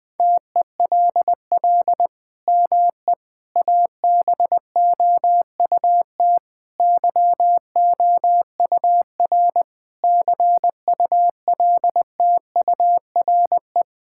Morse code, heard as a keyed tone, about 21 wpm, 700 Hz. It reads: TELL ME ABOUT YOUR CULTURE
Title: Sentences from Top 1000 Words - Review Part 6 - 20wpm